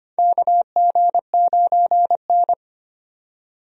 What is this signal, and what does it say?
Morse code, heard as a keyed tone, about 25 wpm, 700 Hz. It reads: KG9N
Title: USA Call Signs - 4 Characters - 25wpm